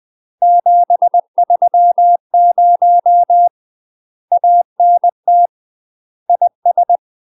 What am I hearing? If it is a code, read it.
Morse code, heard as a keyed tone, about 20 wpm, 700 Hz. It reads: 730 ANT IS
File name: UK MASTER QSOs TRAINING / 20WPM / UK MASTER QSOs 0047